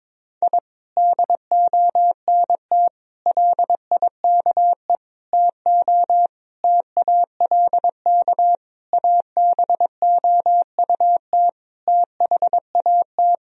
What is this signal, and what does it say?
Morse code, heard as a keyed tone, about 21 wpm, 700 Hz. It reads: I DONT LIKE TO TALK ABOUT THAT
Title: Sentences from Top 1000 Words - Review Part 1 - 22wpm